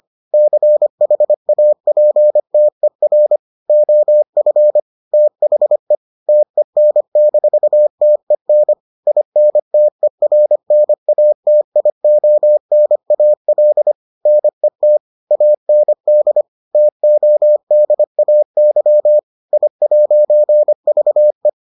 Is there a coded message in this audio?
Yes: Morse code from 0.3 to 21.5 s, heard as a keyed tone, about 25 wpm, 600 Hz. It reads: CHAPTER OF THE TEN-TEN INTERNATIONAL NET AND TODAY I'VE